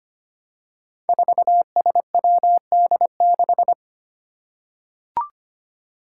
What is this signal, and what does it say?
Morse code, heard as a keyed tone, about 25 wpm, 700 Hz. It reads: 4SWD6
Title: Five Challenging Letter-Number Combinations 25wpm